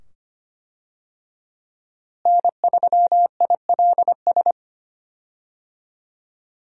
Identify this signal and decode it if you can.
Morse code, heard as a keyed tone, about 25 wpm, 700 Hz. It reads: N3ILS
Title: USA Call Signs - 5 Characters - Repeated 2x - 25wpm